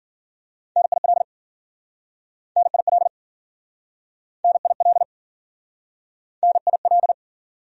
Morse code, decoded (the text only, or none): NIL NIL NIL NIL